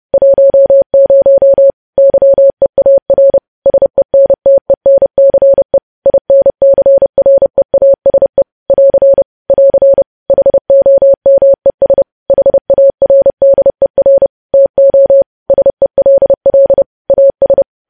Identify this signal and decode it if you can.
Morse code, heard as a keyed tone, about 30 wpm, 550 Hz. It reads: 10 YEAR SENTENCE INCREASE <AR> <AR> HOMES HARDER TO SELL AS
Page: news-headlines-in-morse-code-at-30-wpm